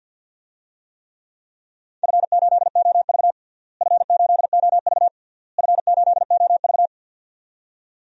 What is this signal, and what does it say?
Morse code, heard as a keyed tone, about 52 wpm, 700 Hz. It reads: F8OV F8OV F8OV
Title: DXCC-100 Call Signs - 4 Characters - Repeated 3x - 50wpm